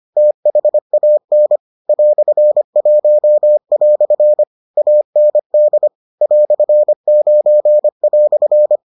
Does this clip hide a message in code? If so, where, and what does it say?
Yes: Morse code from 0.2 to 8.8 s, heard as a keyed tone, about 25 wpm, 600 Hz. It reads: THAN "1" AND "9"